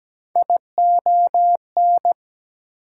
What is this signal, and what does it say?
Morse code, heard as a keyed tone, about 17 wpm, 700 Hz. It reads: ION
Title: Prefixes and Suffixes 17wpm